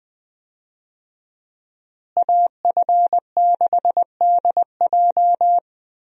Morse code, heard as a keyed tone, about 20 wpm, 700 Hz. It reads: AF6DJ
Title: USA Call Signs - 5 Characters - Repeated 3x - 20wpm